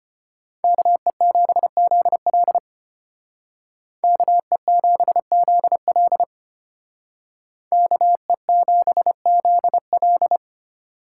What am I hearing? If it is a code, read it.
Morse code, heard as a keyed tone, about 31 wpm, 700 Hz. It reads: KE7ZL KE7ZL KE7ZL